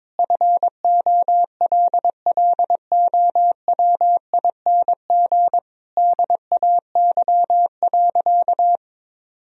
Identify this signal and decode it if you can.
Morse code, heard as a keyed tone, about 22 wpm, 700 Hz. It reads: FOLLOWING DAY.
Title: Sentences from Top 1000 Words - Review Part 4 - 22wpm